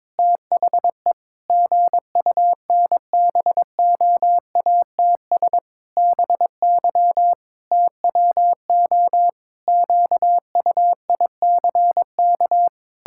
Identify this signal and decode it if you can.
Morse code, heard as a keyed tone, about 21 wpm, 700 Hz. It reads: THE GUNBOATS BY TWO QUICK
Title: Pangram Challenge v2 - 22wpm